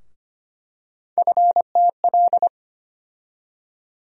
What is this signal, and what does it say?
Morse code, heard as a keyed tone, about 25 wpm, 700 Hz. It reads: FTL
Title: USA Call Sign Postfixes - 3 Letters 25wpm